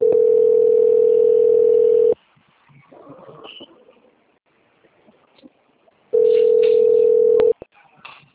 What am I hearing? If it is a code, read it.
Ringback tone (North America), 440 + 480 Hz.